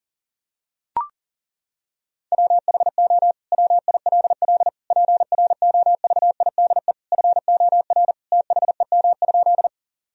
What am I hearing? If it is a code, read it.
Morse code, heard as a keyed tone, about 40 wpm, 700 Hz. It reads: WHO WILL PROVIDE FOR THEM?
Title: Sentences from Top 1000 Words - Review Part 3 - 40wpm